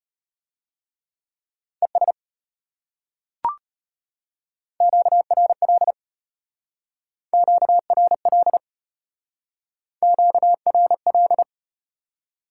Morse code, heard as a keyed tone, about 33 wpm, 700 Hz. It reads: ES QRL QRL QRL